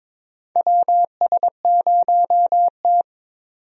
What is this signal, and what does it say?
Morse code, heard as a keyed tone, about 22 wpm, 700 Hz. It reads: WS0T
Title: USA Call Signs - 4 Characters - 22wpm